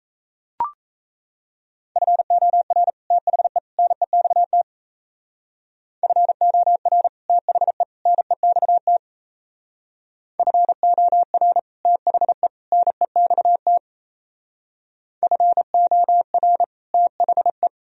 Morse code, heard as a keyed tone, about 36 wpm, 700 Hz. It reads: FOR THE NEXT FOR THE NEXT FOR THE NEXT FOR THE